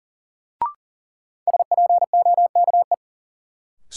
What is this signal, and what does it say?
Morse code, heard as a keyed tone, about 40 wpm, 700 Hz. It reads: SPOKE
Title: Sentences from Top 1200 Words - New Word Drill - 40wpm